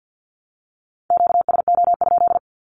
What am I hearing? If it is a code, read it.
Morse code, heard as a keyed tone, about 50 wpm, 700 Hz. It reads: QSO?
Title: Q-Codes with Spoken Meaning - 50wpm